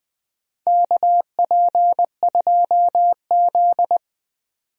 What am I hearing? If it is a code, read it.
Morse code, heard as a keyed tone, about 20 wpm, 700 Hz. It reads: KP2Z
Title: DXCC-100 Call Signs - 4 Characters - 20wpm